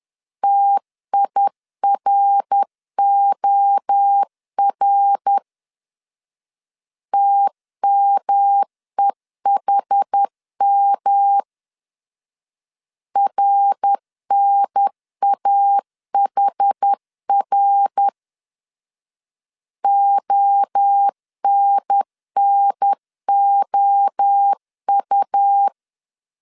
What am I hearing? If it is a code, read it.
Morse code, heard as a keyed tone, about 10 wpm, 800 Hz. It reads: TIROR TMEHM RNAHR ONNOU
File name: CW / DET / 1 3 1500C S B 10wpm 3335